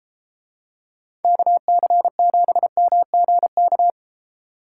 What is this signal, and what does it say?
Morse code, heard as a keyed tone, about 32 wpm, 700 Hz. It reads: KC7MGK